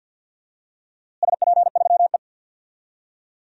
Morse code, heard as a keyed tone, about 50 wpm, 700 Hz. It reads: SP3E
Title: DXCC-100 Call Signs - 4 Characters - Repeated 3x - 50wpm